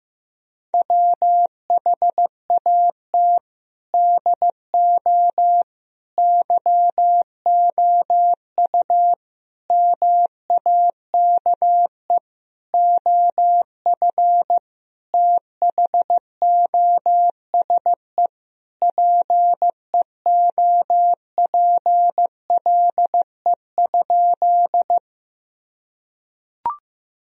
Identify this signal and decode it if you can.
Morse code, heard as a keyed tone, about 15 wpm, 700 Hz. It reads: WHAT DO YOU MAKE OF THOSE PEOPLE?
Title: Sentences from Top 1000 Words - Review Part 1 - 15wpm